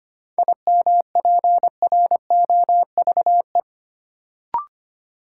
Morse code, heard as a keyed tone, about 25 wpm, 700 Hz. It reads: IMPROVE
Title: Sentences from Top 800 Words - New Word Drill - 25wpm